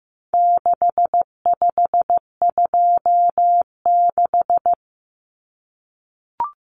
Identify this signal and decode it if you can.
Morse code, heard as a keyed tone, about 15 wpm, 700 Hz. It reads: 6526
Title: Numbers - 4-Digit 15wpm